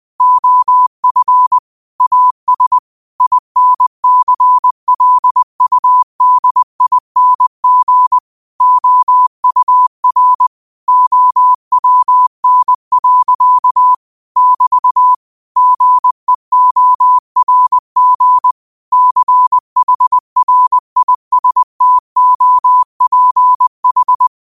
Morse code, heard as a keyed tone, about 20 wpm, 1000 Hz. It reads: OF AS INCLUDING OUR OWN. <BT> GEORG CHRISTOPH
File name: 2025-06-25-20wpm